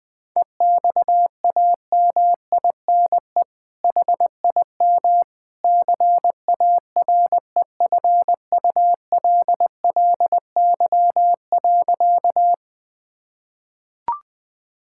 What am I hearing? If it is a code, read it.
Morse code, heard as a keyed tone, about 21 wpm, 700 Hz. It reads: EXAMINE HIM CAREFULLY.
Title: Sentences from Top 1600 Words 20wpm